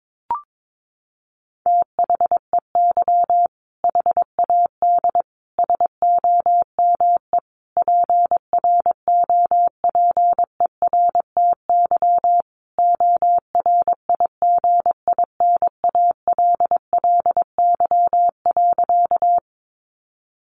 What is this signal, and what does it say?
Morse code, heard as a keyed tone, about 22 wpm, 700 Hz. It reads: THEY HAD SOME PROPERTY ORIGINALLY.
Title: Sentences from Top 1500 Words 22wpm